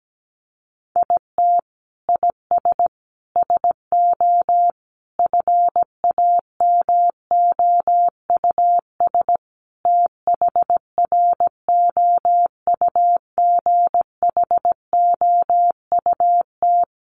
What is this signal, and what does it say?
Morse code, heard as a keyed tone, about 17 wpm, 700 Hz. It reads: IT IS SO FAMOUS THROUGHOUT
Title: Sentences from Top 1500 Words 17wpm